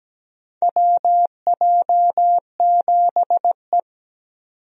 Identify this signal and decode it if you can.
Morse code, heard as a keyed tone, about 17 wpm, 700 Hz. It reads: WJ7E
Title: USA Call Signs - 4 Characters - 17wpm